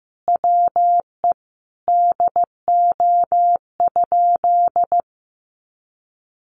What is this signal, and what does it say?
Morse code, heard as a keyed tone, about 15 wpm, 700 Hz. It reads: WE DO?
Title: Sentences from Top 1300 Words 15wpm